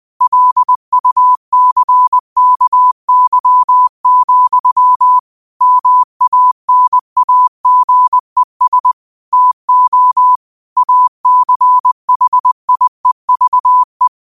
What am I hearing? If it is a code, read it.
Morse code, heard as a keyed tone, about 20 wpm, 1000 Hz. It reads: LUCKY, MANAGES TO ACHIEVE